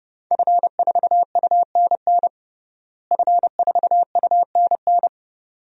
Morse code, heard as a keyed tone, about 30 wpm, 700 Hz. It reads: F4UNN F4UNN
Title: DXCC-100 Call Signs - 5 Characters - Repeated 2x - 30wpm